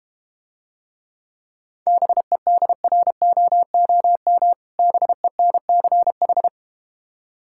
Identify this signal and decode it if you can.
Morse code, heard as a keyed tone, about 32 wpm, 700 Hz. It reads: BEDROOM BENCH